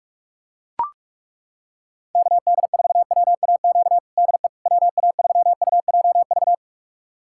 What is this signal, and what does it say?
Morse code, heard as a keyed tone, about 41 wpm, 700 Hz. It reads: KD4WAQ DE WA3UJV
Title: USA Callsign DE Callsign 45wpm